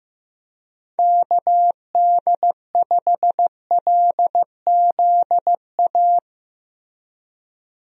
Morse code, heard as a keyed tone, about 15 wpm, 700 Hz. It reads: KD5LZA
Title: USA Call Signs - 6 Characters - Repeated 3x - 15wpm